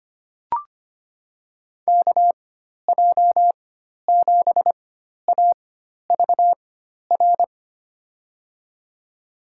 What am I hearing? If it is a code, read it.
Morse code, heard as a keyed tone, about 25 wpm, spaced as for 12 wpm, 700 Hz. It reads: KJ7AVR
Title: USA Call Signs - 6 Characters - 4x Character Spacing - 25wpm